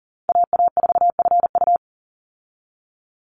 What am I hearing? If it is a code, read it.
Morse code, heard as a keyed tone, about 40 wpm, 700 Hz. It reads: AA4FU
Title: Call Signs Repeated 2x - 40wpm